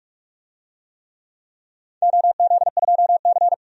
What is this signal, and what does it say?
Morse code, heard as a keyed tone, about 44 wpm, 700 Hz. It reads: OZ2C
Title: DXCC-100 Call Signs - 4 Characters - 45wpm